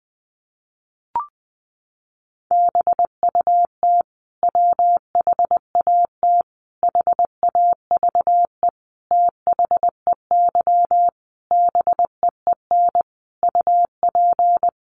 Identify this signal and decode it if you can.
Morse code, heard as a keyed tone, about 20 wpm, 700 Hz. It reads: BUT WHAT HAVE THEY BEEN UP
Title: Sentences from Top 100 Words 20wpm